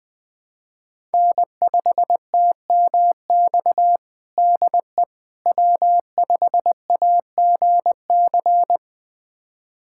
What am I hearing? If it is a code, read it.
Morse code, heard as a keyed tone, about 20 wpm, 700 Hz. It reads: N5TMX DE W5AGC
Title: USA Callsign DE Callsign - 5 Characters - 20wpm